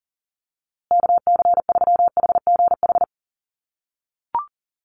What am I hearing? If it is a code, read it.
Morse code, heard as a keyed tone, about 40 wpm, 700 Hz. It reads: KC3HGH
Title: USA Call Signs Repeated 2x 40wpm